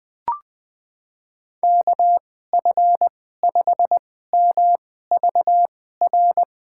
Morse code, heard as a keyed tone, about 21 wpm, spaced as for 15 wpm, 700 Hz. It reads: KF5MVR
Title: USA Call Signs - 6 Characters - 2x Character Spacing - 20wpm